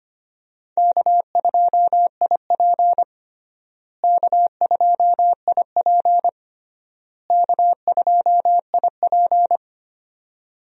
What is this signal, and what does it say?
Morse code, heard as a keyed tone, about 25 wpm, 700 Hz. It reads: K2IP K2IP K2IP
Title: USA Call Signs - 4 Characters - Repeated 3x - 25wpm